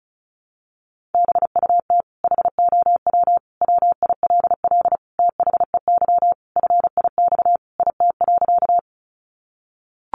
Morse code, heard as a keyed tone, about 34 wpm, 700 Hz. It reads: BUT HOW WILL THEY FIX IT.
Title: Sentences from Top 1400 Words 35wpm